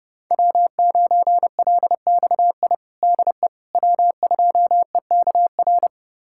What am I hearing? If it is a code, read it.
Morse code, heard as a keyed tone, about 29 wpm, 700 Hz. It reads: W9LXI DE W2EKR
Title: USA Callsign DE Callsign - 5 Characters - 30wpm